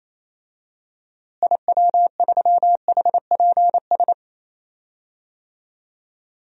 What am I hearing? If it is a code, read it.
Morse code, heard as a keyed tone, about 28 wpm, 700 Hz. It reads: IW3HPS
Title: DXCC-100 Call Signs - 6 Characters - Repeated 2x - 28wpm